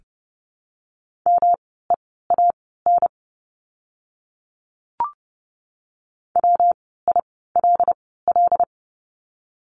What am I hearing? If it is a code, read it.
Morse code, heard as a keyed tone, about 30 wpm, spaced as for 17 wpm, 700 Hz. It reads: MEAN WILL
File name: course / 30:10wpm / 115 Lesson 19f - Letter w - Words 30wpm